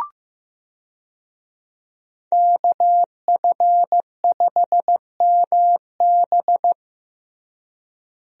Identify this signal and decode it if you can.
Morse code, heard as a keyed tone, about 15 wpm, 700 Hz. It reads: KF5MB
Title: USA Call Signs - 5 Characters - Repeated 2x - 15wpm